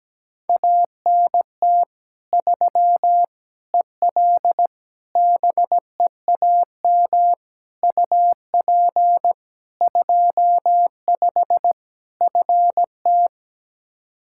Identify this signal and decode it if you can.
Morse code, heard as a keyed tone, about 17 wpm, 700 Hz. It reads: ANT 3 EL BEAM UP 25 FT
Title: K1USN SST QSO Practice 17wpm